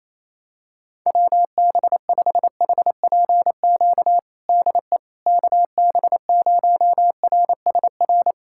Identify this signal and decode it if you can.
Morse code, heard as a keyed tone, about 28 wpm, 700 Hz. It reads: WB5HPQ DE KB0RSR
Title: USA Callsign DE Callsign - 6 Characters - 28wpm